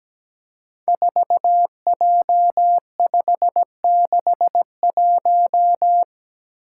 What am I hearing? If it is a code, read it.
Morse code, heard as a keyed tone, about 17 wpm, 700 Hz. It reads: 4J561
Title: Five Challenging Letter-Number Combinations 17wpm